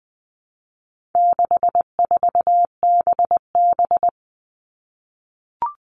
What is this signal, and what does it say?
Morse code, heard as a keyed tone, about 20 wpm, 700 Hz. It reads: 64BB
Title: All Cut Numbers - 4-Digit 20wpm